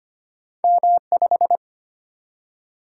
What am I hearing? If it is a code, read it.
Morse code, heard as a keyed tone, about 25 wpm, 700 Hz. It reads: M5
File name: Two Letter-Number Combinations - 25wpm